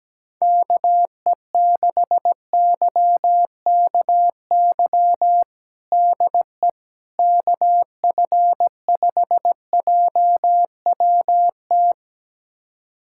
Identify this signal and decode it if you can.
Morse code, heard as a keyed tone, about 17 wpm, 700 Hz. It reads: KE6YKY DE KF5JWT
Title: USA Callsign DE Callsign - 6 Characters - 17wpm